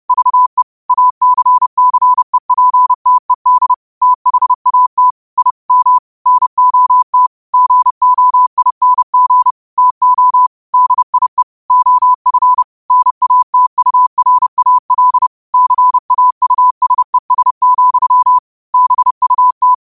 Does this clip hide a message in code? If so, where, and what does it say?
Yes: Morse code from 0.1 to 19.7 s, heard as a keyed tone, about 30 wpm, 1000 Hz. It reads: VE ACCEPTED THAT I M NOT GOING TO DIE OF NATURAL CAUSES, BUT